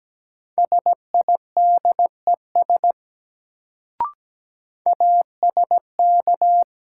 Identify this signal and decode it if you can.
Morse code, heard as a keyed tone, about 17 wpm, 700 Hz. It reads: SIDES ASK
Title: Sets of 4 Words - New Word Drill - 17wpm